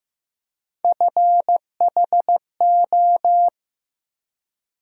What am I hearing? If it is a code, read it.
Morse code, heard as a keyed tone, about 15 wpm, 700 Hz. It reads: FHO
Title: USA Call Sign Postfixes - 3 Letters 15wpm